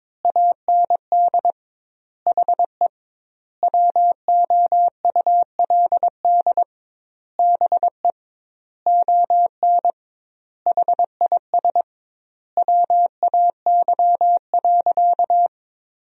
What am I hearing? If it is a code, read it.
Morse code, heard as a keyed tone, about 22 wpm, 700 Hz. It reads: AND HE WOULD BE ON HIS WAY.
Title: 2x Word-Spacing - Sentences from Top 300 Words - 22wpm